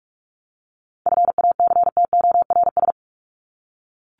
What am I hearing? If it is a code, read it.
Morse code, heard as a keyed tone, about 44 wpm, 700 Hz. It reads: FACTORS
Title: ES:S Suffix - Top 100 Words with Base Words 45wpm